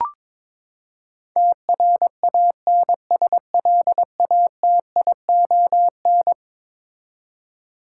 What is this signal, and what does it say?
Morse code, heard as a keyed tone, about 23 wpm, 700 Hz. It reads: TRANSLATION
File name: practice / 22wpm / Eleven Letter Words - 22wpm